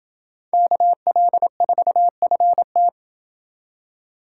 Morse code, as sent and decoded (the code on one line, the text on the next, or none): -.- .-.. ....- ..-. -
KL4FT